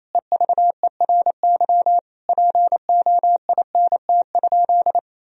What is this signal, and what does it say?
Morse code, heard as a keyed tone, about 28 wpm, 700 Hz. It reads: EVERY POINT?
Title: Sentences from Top 1000 Words - Review Part 1 - 28wpm